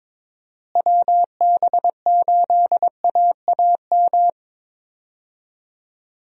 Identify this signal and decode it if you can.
Morse code, heard as a keyed tone, about 22 wpm, 700 Hz. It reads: WB8AAM
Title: USA Call Signs - 6 Characters - Repeated 2x - 22wpm